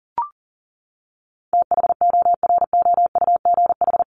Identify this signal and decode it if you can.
Morse code, heard as a keyed tone, about 40 wpm, 700 Hz. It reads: THOROUGH
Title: LY Suffix - Top 100 Words with Base Words 40wpm